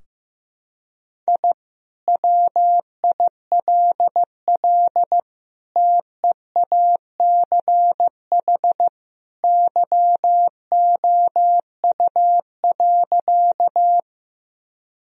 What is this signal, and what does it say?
Morse code, heard as a keyed tone, about 15 wpm, 700 Hz. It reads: I WILL TEACH YOU.
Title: Sentences from Top 1000 Words - Review Part 2 - 15wpm